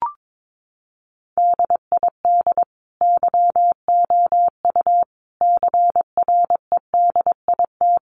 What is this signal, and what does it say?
Morse code, heard as a keyed tone, about 22 wpm, 700 Hz. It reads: DID YOU CREDIT
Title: Sentences from Top 1000 Words 22wpm